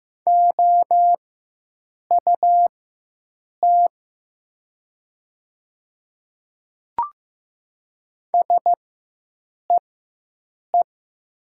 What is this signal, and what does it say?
Morse code, heard as a keyed tone, about 15 wpm, spaced as for 7 wpm, 700 Hz. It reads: OUT SEE
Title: 4x Character Spacing - Top 200 Words - 15wpm